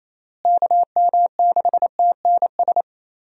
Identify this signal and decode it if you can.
Morse code, heard as a keyed tone, about 28 wpm, 700 Hz. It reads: KM6TNS
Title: USA Call Signs - 6 Characters - Repeated 2x - 28wpm